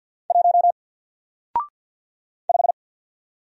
Morse code, heard as a keyed tone, about 48 wpm, 700 Hz. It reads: 1 5
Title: Single Letter-Number Alt 50wpm